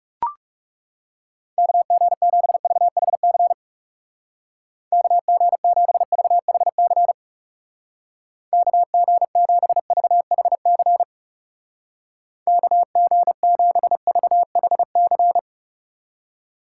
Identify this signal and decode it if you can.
Morse code, heard as a keyed tone, about 36 wpm, 700 Hz. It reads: KG7VHC KG7VHC KG7VHC KG7VHC
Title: Speed Racing - USA 2x3 Call Signs 30wpm